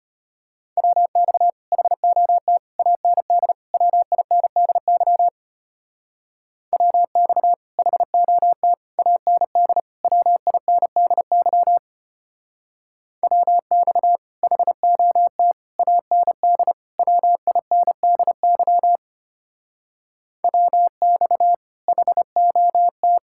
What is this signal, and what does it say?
Morse code, heard as a keyed tone, about 30 wpm, 700 Hz. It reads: WX HOT AND WINDY WX HOT AND WINDY WX HOT AND WINDY WX HOT
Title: Speed-Racing Common QSO Phrases - 25wpm